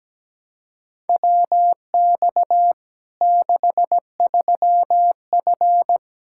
Morse code, heard as a keyed tone, about 17 wpm, 700 Hz. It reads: WX 63F
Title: Practice QSO - Exchange 2 - Info - 17wpm